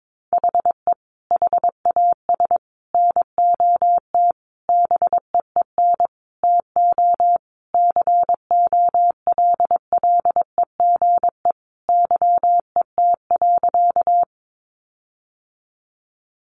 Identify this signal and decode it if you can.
Morse code, heard as a keyed tone, about 22 wpm, 700 Hz. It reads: HE HAS NOT BEEN TO COLLEGE YET.
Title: Sentences from Top 1300 Words 22wpm